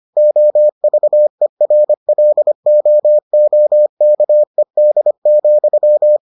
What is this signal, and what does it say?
Morse code, heard as a keyed tone, about 25 wpm, 600 Hz. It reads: OVERLOOKED,